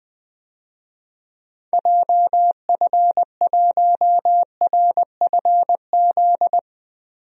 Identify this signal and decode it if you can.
Morse code, heard as a keyed tone, about 20 wpm, 700 Hz. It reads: JF1RFZ
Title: DXCC-100 Call Signs - 6 Characters - Repeated 2x - 20wpm